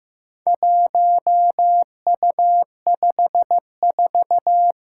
Morse code, heard as a keyed tone, about 15 wpm, 700 Hz. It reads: 1U54
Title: Four Challenging Letter-Number Combinations 15wpm